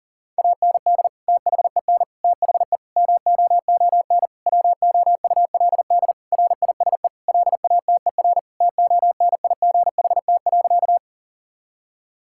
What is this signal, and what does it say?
Morse code, heard as a keyed tone, about 40 wpm, 700 Hz. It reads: AND THEN THE MOON WOULD RISE LATER TONIGHT.
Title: Sentences from Top 1000 Words - Review Part 6 - 40wpm